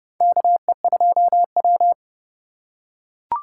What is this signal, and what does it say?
Morse code, heard as a keyed tone, about 30 wpm, 700 Hz. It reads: KE2W
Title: USA Call Signs - 4 Characters - 4x Character Spacing - 30wpm